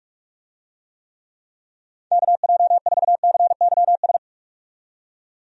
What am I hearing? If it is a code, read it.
Morse code, heard as a keyed tone, about 41 wpm, 700 Hz. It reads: KJ4CYS
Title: USA Call Signs - 6 Characters - 45wpm